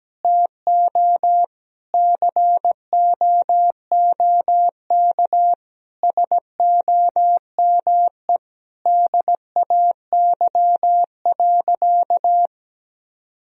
Morse code, heard as a keyed tone, about 17 wpm, 700 Hz. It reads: TO COOK SOME DAY.
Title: Sentences from Top 1000 Words - Review Part 4 - 17wpm